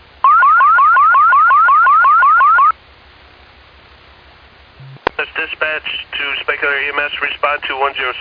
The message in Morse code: none

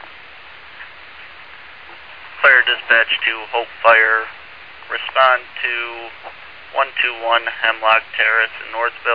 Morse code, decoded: none

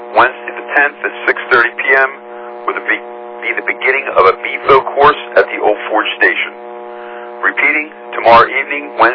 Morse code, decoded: none

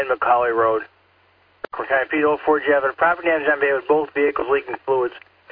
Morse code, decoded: none

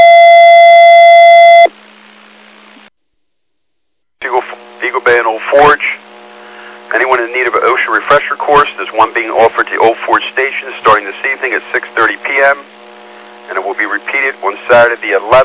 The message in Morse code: none

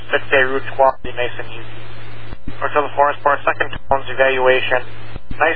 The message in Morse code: none

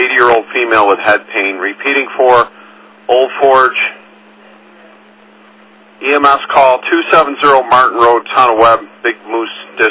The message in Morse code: none

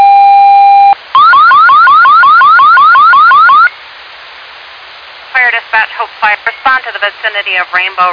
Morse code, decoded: none

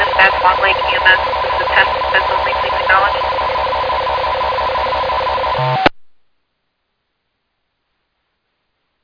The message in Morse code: none

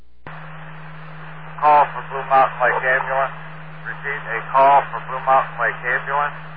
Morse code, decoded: none

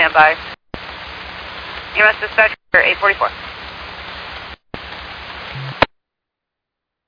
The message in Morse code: none

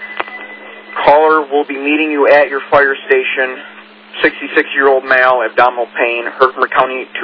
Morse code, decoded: none

